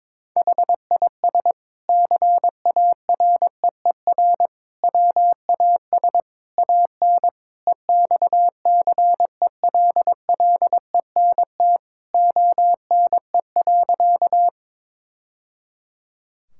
Morse code, heard as a keyed tone, about 22 wpm, 700 Hz. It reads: HIS CAREER WAS AN EXCELLENT ONE.
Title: Sentences from Top 1000 Words - Review Part 6 - 22wpm